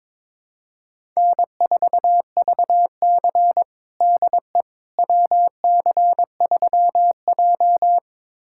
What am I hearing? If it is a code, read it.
Morse code, heard as a keyed tone, about 21 wpm, 700 Hz. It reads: N4VC DE WC3J
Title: USA Callsign DE Callsign - 4 Characters - 22wpm